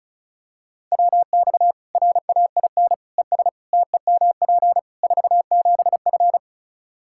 Morse code, heard as a keyed tone, about 36 wpm, 700 Hz. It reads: WX RAIN ES TEMP 47F
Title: Practice QSO - Exchange 2 - Info - 35wpm